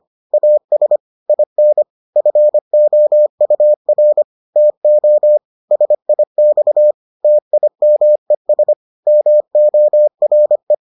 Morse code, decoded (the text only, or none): AS IN FOUR TO SIX TIMES MORE